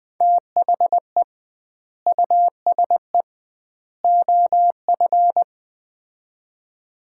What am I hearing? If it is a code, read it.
Morse code, heard as a keyed tone, about 20 wpm, 700 Hz. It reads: THE USE OF
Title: 2x Word-Spacing - Sets of 3 Words - 20wpm